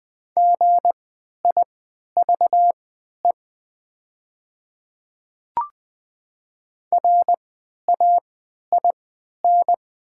Morse code, heard as a keyed tone, about 20 wpm, spaced as for 11 wpm, 700 Hz. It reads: GIVE RAIN
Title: 3x Character Spacing - Top 300 Words - 20wpm